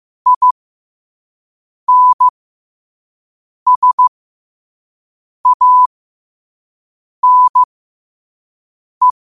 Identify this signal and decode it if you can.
Morse code, heard as a keyed tone, about 15 wpm, spaced as for 5 wpm, 1000 Hz. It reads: INSANE